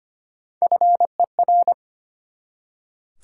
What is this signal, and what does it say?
Morse code, heard as a keyed tone, about 33 wpm, 700 Hz. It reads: FER